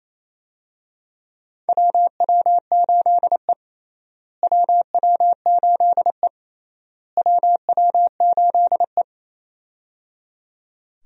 Morse code, heard as a keyed tone, about 27 wpm, 700 Hz. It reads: WW8E WW8E WW8E
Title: USA Call Signs - 4 Characters - Repeated 3x - 28wpm